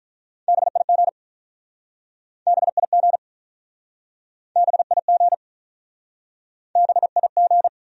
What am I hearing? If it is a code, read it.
Morse code, heard as a keyed tone, about 47 wpm, 700 Hz. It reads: BIG BIG BIG BIG